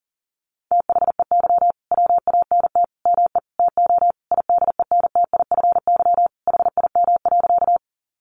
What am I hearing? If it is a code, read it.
Morse code, heard as a keyed tone, about 40 wpm, 700 Hz. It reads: THEY WANT ME TO IDENTIFY HIM.
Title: Sentences from Top 900 Words 40wpm